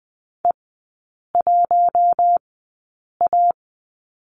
Morse code, heard as a keyed tone, about 20 wpm, spaced as for 8 wpm, 700 Hz. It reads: E1A
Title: 087 Lesson 15e - Number 5 - Three Characters 20wpm